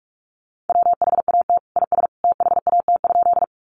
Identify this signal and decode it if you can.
Morse code, heard as a keyed tone, about 44 wpm, 700 Hz. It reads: WHAT IS THAT?
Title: Sentences from Top 100 Words 45wpm